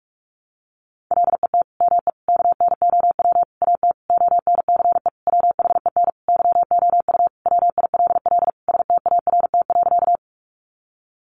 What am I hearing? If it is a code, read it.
Morse code, heard as a keyed tone, about 44 wpm, 700 Hz. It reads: LET ME KNOW AT ONCE WHEN YOU WILL START.
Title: Sentences from Top 300 Words 45wpm